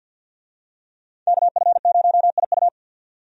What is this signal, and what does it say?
Morse code, heard as a keyed tone, about 50 wpm, 700 Hz. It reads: KF0IU
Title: USA Call Signs - 5 Characters - 50wpm